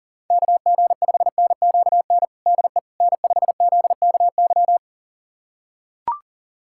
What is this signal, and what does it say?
Morse code, heard as a keyed tone, about 40 wpm, 700 Hz. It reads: KG5NQN DE N5ZKY
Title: USA Callsign DE Callsign 40wpm